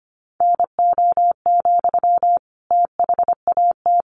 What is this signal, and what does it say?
Morse code, heard as a keyed tone, about 25 wpm, 700 Hz. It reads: NO, THAT
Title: Sentences from Top 300 Words 25wpm